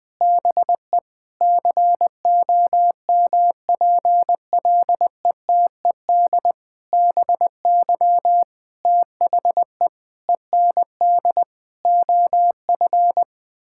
Morse code, heard as a keyed tone, about 20 wpm, 700 Hz. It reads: BE COMPLETED BY THE END OF
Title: Sentences with 8-Word Phrases - 20wpm